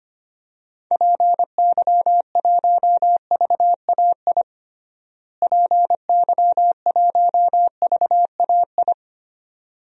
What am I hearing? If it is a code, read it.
Morse code, heard as a keyed tone, about 25 wpm, 700 Hz. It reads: PY1VAI PY1VAI
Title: DXCC-100 Call Signs - 6 Characters - Repeated 2x - 25wpm